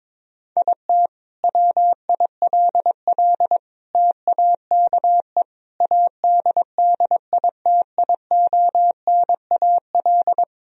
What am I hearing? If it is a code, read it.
Morse code, heard as a keyed tone, about 22 wpm, 700 Hz. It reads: IT WILL TAKE ADDITIONAL
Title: Sentences from Top 1000 Words - Review Part 4 - 22wpm